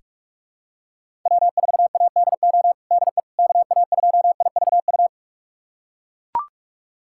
Morse code, heard as a keyed tone, about 42 wpm, 700 Hz. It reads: W4ADO DE KA2IVU
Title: USA Callsign DE Callsign 45wpm